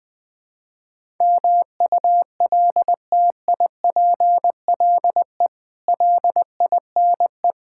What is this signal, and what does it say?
Morse code, heard as a keyed tone, about 21 wpm, 700 Hz. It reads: MULTIPLE LINE
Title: Build A Phrase - 5-Word Phrases - 20wpm